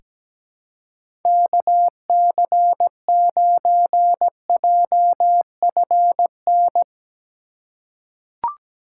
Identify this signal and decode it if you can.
Morse code, heard as a keyed tone, about 17 wpm, 700 Hz. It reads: KC9JFN
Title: USA Call Signs - 2x Character Spacing - 17wpm